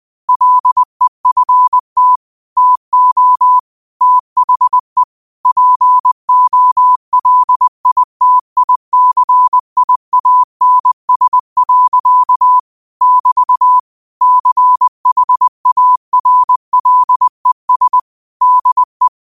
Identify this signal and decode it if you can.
Morse code, heard as a keyed tone, about 20 wpm, 1000 Hz. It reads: LEFT TO THE POLITICIANS. <BT> CHARLES DE